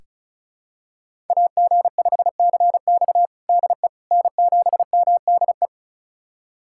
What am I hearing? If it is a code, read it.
Morse code, heard as a keyed tone, about 36 wpm, 690 Hz. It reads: AG5CX DE N7MDE
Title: USA Callsign DE Callsign - 5 Characters - 35wpm